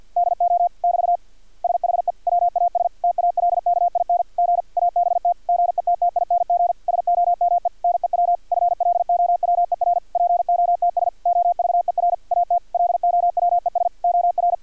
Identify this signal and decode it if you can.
Morse code, heard as a keyed tone, about 50 wpm, 700 Hz. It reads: DO <BT> SHE WAS TALKING ABT GETTING SOME NEW FLOWER POTS OVER AT LOWES OR
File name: archived / CRSnet